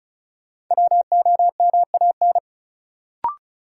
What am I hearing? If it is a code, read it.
Morse code, heard as a keyed tone, about 34 wpm, 700 Hz. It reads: WOMAN
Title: Sentences from Top 1100 Words - New Word Drill Alt - 35wpm